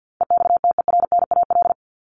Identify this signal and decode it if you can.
Morse code, heard as a keyed tone, about 50 wpm, 700 Hz. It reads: EXTERNAL
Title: Top 1751-2000 Words 50wpm